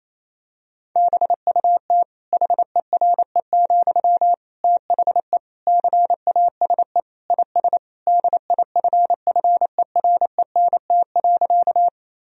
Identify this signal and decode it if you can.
Morse code, heard as a keyed tone, about 27 wpm, 700 Hz. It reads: BUT HERE, THE CASE IS DIFFERENT.
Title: Sentences from Top 1000 Words - Review Part 3 - 28wpm